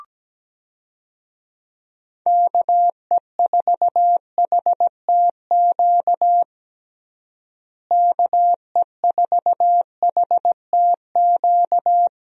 Morse code, heard as a keyed tone, about 17 wpm, 700 Hz. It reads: KE4HTQ KE4HTQ
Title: USA Call Signs - 6 Characters - Repeated 2x - 17wpm